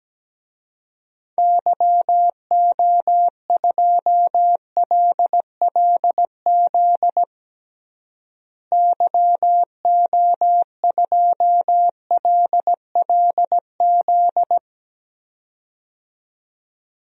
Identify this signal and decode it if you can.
Morse code, heard as a keyed tone, about 17 wpm, 700 Hz. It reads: YO2LLZ YO2LLZ
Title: DXCC-100 Call Signs - 6 Characters - Repeated 2x - 17wpm